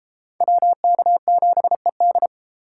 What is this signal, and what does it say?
Morse code, heard as a keyed tone, about 30 wpm, 700 Hz. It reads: WK7ED